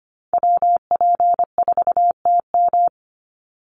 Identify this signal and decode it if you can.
Morse code, heard as a keyed tone, about 25 wpm, 700 Hz. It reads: WP4TM
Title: USA Call Signs 25wpm